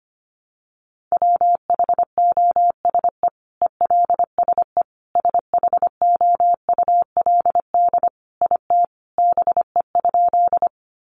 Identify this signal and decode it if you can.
Morse code, heard as a keyed tone, about 25 wpm, 700 Hz. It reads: WHOSE ELSE SHOULD IT BE?
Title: Sentences from Top 1200 Words 25wpm